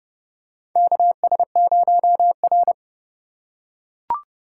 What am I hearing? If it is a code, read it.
Morse code, heard as a keyed tone, about 31 wpm, 700 Hz. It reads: KS0R
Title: USA Call Signs - 2x Character Spacing - 30wpm